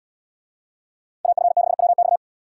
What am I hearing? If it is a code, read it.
Morse code, heard as a keyed tone, about 70 wpm, 700 Hz. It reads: AF6WX